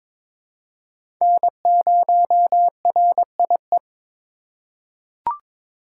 Morse code, heard as a keyed tone, about 21 wpm, 700 Hz. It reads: N0RIE
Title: USA Call Signs - 5 Characters - 3x Character Spacing - 22wpm